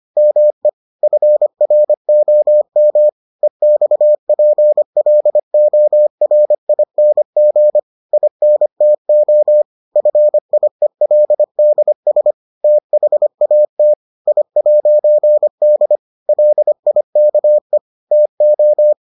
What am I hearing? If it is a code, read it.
Morse code, heard as a keyed tone, about 25 wpm, 600 Hz. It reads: ME FROM EXPLORING INTO FIELDS THAT I'D LIKE TO